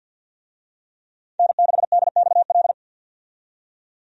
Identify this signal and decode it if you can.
Morse code, heard as a keyed tone, about 48 wpm, 700 Hz. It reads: N6DXL
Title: USA Call Signs - 5 Characters - 50wpm